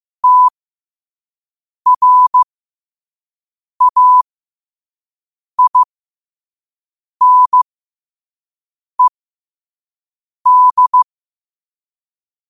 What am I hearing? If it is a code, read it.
Morse code, heard as a keyed tone, about 15 wpm, spaced as for 5 wpm, 1000 Hz. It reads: TRAINED
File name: qotdfiles / 5wpm